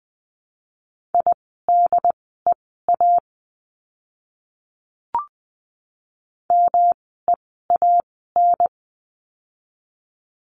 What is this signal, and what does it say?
Morse code, heard as a keyed tone, about 20 wpm, spaced as for 15 wpm, 700 Hz. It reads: IDEA MEAN
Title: 108 Lesson 18f - Letter m - Words 20wpm